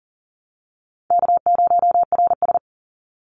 Morse code, heard as a keyed tone, about 42 wpm, 700 Hz. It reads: K0RS
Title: NA SOTA Chaser Callsigns 40wpm